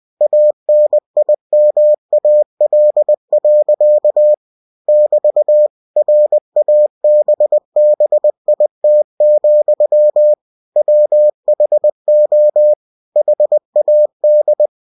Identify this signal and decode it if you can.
Morse code, heard as a keyed tone, about 20 wpm, 600 Hz. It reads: ANIMAL. <BT> RABBIT, WHO HAD
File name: cw-books / winnie-the-pooh / winnie-20wpm-page-0062